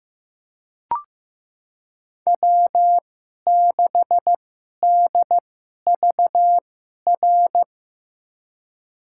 Morse code, heard as a keyed tone, about 16 wpm, spaced as for 11 wpm, 700 Hz. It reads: W6DVR
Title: USA Call Signs - 2x Character Spacing - 15wpm